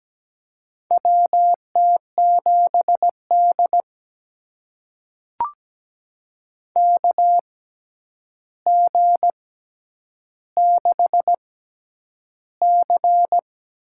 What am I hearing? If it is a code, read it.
Morse code, heard as a keyed tone, about 17 wpm, spaced as for 9 wpm, 700 Hz. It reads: WT7D KG6C
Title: USA Call Signs - 6x Character Spacing - 17wpm